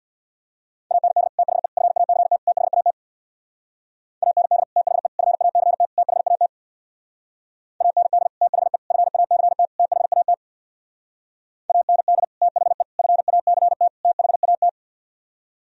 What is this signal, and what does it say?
Morse code, heard as a keyed tone, about 64 wpm, 700 Hz. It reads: AND THE FACT THAT AND THE FACT THAT AND THE FACT THAT AND THE FACT THAT